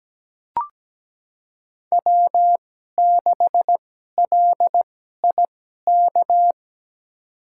Morse code, heard as a keyed tone, about 17 wpm, spaced as for 12 wpm, 700 Hz. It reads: W6LIK
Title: USA Call Signs - 5 Characters - 2x Character Spacing - 17wpm